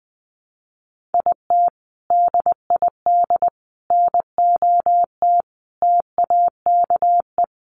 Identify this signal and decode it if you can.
Morse code, heard as a keyed tone, about 19 wpm, 700 Hz. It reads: IT DID NOT TAKE
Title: Sets of 5 Words Encore2 20wpm